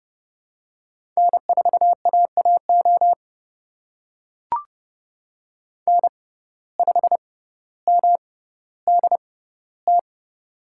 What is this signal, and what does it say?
Morse code, heard as a keyed tone, about 31 wpm, spaced as for 15 wpm, 700 Hz. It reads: N4AAO N5MDT
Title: USA Call Signs - 5 Characters - 6x Character Spacing - 30wpm